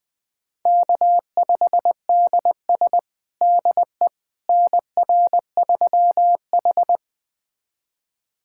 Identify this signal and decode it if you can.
Morse code, heard as a keyed tone, about 21 wpm, 700 Hz. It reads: K5DS DE NR3H
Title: USA Callsign DE Callsign - 4 Characters - 20wpm